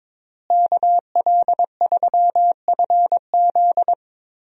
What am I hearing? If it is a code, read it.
Morse code, heard as a keyed tone, about 22 wpm, 700 Hz. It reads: KL3FZ
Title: USA Call Signs - 5 Characters - Repeated 3x - 22wpm